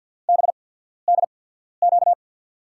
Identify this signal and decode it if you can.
Morse code, heard as a keyed tone, about 50 wpm, spaced as for 14 wpm, 700 Hz. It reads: BDQ
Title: USA Call Signs - 6 Characters - 8x Character Spacing - 50wpm